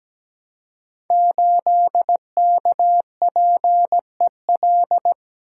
Morse code, heard as a keyed tone, about 17 wpm, 700 Hz. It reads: 8KPEL
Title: Code Groups - 5 Letter:Numbers - 17wpm